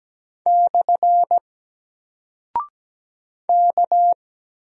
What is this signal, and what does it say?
Morse code, heard as a keyed tone, about 17 wpm, 700 Hz. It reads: / K
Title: Single Character (Letter:Number:Punctuation:Prosign) 17wpm